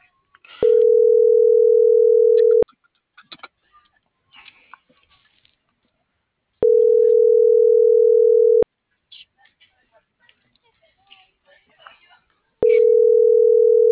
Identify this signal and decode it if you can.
Ringback tone (North America), 440 + 480 Hz.
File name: CENTRAL MULTIMODAL / out alfredovelazquez